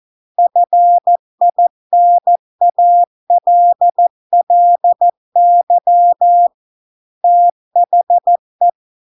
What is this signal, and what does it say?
Morse code, heard as a keyed tone, about 14 wpm, 700 Hz. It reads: FINALLY THE